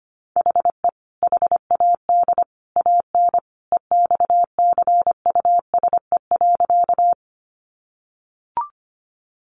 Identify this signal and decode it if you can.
Morse code, heard as a keyed tone, about 25 wpm, 700 Hz. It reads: HE HAD AN EXCUSE.
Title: Sentences from Top 800 Words 25wpm